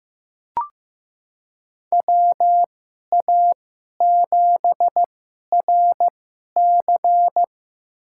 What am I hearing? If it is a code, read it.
Morse code, heard as a keyed tone, about 15 wpm, spaced as for 11 wpm, 700 Hz. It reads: WA7RC
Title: USA Call Signs - 2x Character Spacing - 15wpm